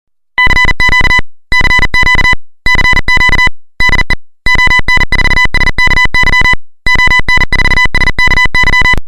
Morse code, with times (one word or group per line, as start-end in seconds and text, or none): 0.4-1.2 CQ
1.5-2.3 CQ
2.7-3.5 CQ
3.8-4.1 DE
4.5-6.5 ON4SKY
6.9-8.9 ON4SKY